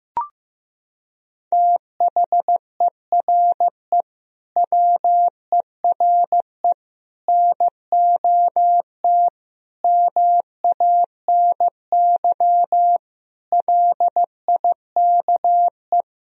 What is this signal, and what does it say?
Morse code, heard as a keyed tone, about 15 wpm, 700 Hz. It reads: THERE WERE NOT MANY LIKE
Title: Sentences from Top 1000 Words - Review Part 1 - 15wpm